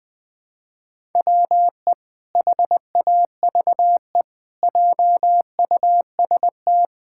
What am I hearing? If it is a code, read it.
Morse code, heard as a keyed tone, about 19 wpm, 700 Hz. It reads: WE HAVE JUST
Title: Pangram Challenge v2 - 20wpm